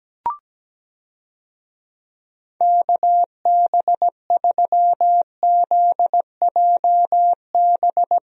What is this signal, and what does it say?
Morse code, heard as a keyed tone, about 17 wpm, 700 Hz. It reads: KB3ZJB
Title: USA Call Signs - 6 Characters - Repeated 3x - 17wpm